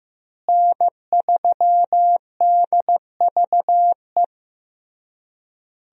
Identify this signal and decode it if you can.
Morse code, heard as a keyed tone, about 15 wpm, 700 Hz. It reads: N3DVE
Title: USA Call Signs - 5 Characters - Repeated 3x - 15wpm